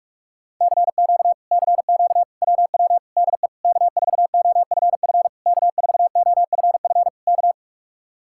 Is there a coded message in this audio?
Yes: Morse code from 0.6 to 7.5 s, heard as a keyed tone, about 49 wpm, 700 Hz. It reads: CQ CQ WW DE K4OFF K4OFF K